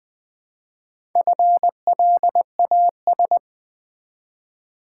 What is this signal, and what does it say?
Morse code, heard as a keyed tone, about 20 wpm, 700 Hz. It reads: FLAS
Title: Build A Word - 1 Letter At A Time - 3 to 5-Letter Words - 20wpm